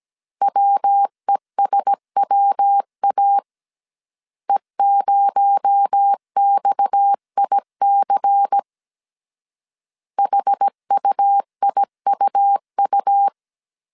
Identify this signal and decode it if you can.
Morse code, heard as a keyed tone, about 17 wpm, 800 Hz. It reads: WESWA E0XIC HUIUU
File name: 16wpm